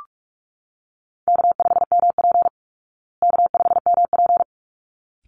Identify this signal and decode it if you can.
Morse code, heard as a keyed tone, about 44 wpm, 700 Hz. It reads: K5MP K5MP
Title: Easy Call Signs Repeated 2x - 45wpm